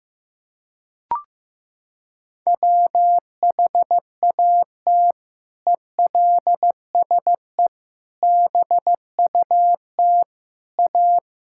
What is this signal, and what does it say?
Morse code, heard as a keyed tone, about 15 wpm, 700 Hz. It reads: WHAT ELSE BUT A
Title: Sentences from Top 1600 Words 15wpm